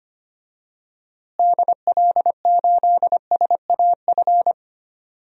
Morse code, heard as a keyed tone, about 25 wpm, 700 Hz. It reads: DL8SAF
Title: DXCC-100 Call Signs - 6 Characters - Repeated 3x - 25wpm